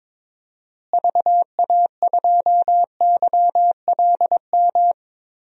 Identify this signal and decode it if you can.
Morse code, heard as a keyed tone, about 22 wpm, 700 Hz. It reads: VA2YLM
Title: DXCC-100 Call Signs - 6 Characters - Repeated 2x - 22wpm